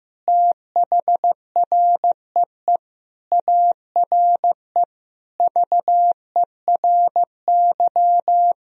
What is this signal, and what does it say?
Morse code, heard as a keyed tone, about 15 wpm, 700 Hz. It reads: THREE ARE VERY
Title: Sentences from Top 1000 Words - Review Part 1 - 15wpm